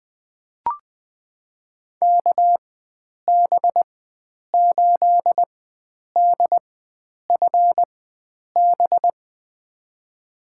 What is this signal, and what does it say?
Morse code, heard as a keyed tone, about 20 wpm, spaced as for 9 wpm, 700 Hz. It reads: KB8DFB
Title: USA Call Signs - 6 Characters - 4x Character Spacing - 20wpm